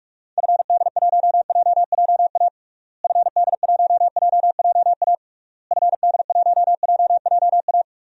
Morse code, heard as a keyed tone, about 41 wpm, 700 Hz. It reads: FD1JJA FD1JJA FD1JJA
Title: DXCC-100 Call Signs - 6 Characters - Repeated 3x - 45wpm